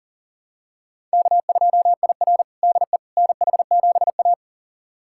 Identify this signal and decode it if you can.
Morse code, heard as a keyed tone, about 40 wpm, 700 Hz. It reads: K2IR DE NH7A
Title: USA Callsign DE Callsign - 4 Characters - 40wpm